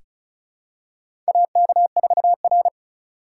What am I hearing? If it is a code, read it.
Morse code, heard as a keyed tone, about 34 wpm, 700 Hz. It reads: AK4R
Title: USA Call Signs - 4 Characters - 2x Character Spacing - 35wpm